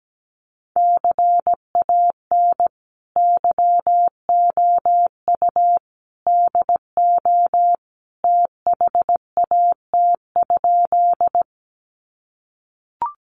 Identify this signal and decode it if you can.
Morse code, heard as a keyed tone, about 17 wpm, 700 Hz. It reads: CAN YOU DO THAT?
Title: Sentences from Top 100 Words 17wpm